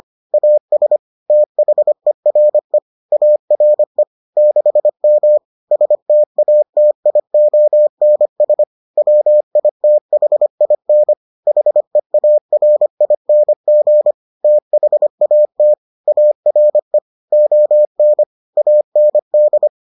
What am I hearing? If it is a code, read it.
Morse code, heard as a keyed tone, about 25 wpm, 600 Hz. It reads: AS THERE ARE 6M STATIONS WITHIN HEARING THAT ARE ON AND